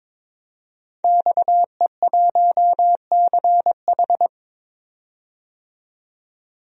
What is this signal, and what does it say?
Morse code, heard as a keyed tone, about 22 wpm, 700 Hz. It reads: XE1CH
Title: DXCC-100 Call Signs - 5 Characters - 22wpm